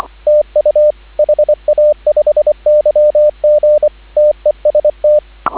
Morse code, heard as a keyed tone, about 24 wpm, 590 Hz, with radio noise. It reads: TU HA5YG TEST
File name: CW / 2017 HA-OB HA5YG